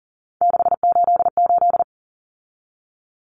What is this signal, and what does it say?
Morse code, heard as a keyed tone, about 40 wpm, 700 Hz. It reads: 688
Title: Numbers - 3-Digit 40wpm